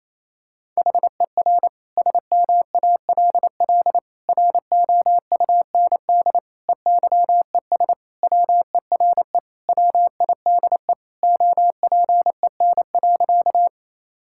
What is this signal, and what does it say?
Morse code, heard as a keyed tone, about 27 wpm, 700 Hz. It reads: HER SMALL ROUND EYES WERE WIDE OPEN.
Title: Sentences from Top 1000 Words - Review Part 6 - 28wpm